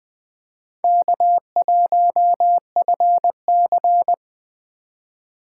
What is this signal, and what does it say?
Morse code, heard as a keyed tone, about 21 wpm, 700 Hz. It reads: K1FC